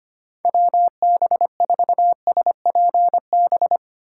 Morse code, heard as a keyed tone, about 25 wpm, 700 Hz. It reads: WB4SPB
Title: K1USN SST QSO Practice - 25wpm